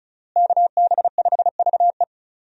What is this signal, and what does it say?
Morse code, heard as a keyed tone, about 36 wpm, 700 Hz. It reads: KB5VE
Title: USA Call Signs - 5 Characters - Repeated 3x - 35wpm